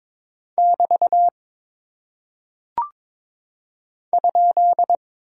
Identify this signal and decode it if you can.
Morse code, heard as a keyed tone, about 22 wpm, 700 Hz. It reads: <BT> ?
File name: practice / 22wpm / Single Punctuation-Prosign - 22wpm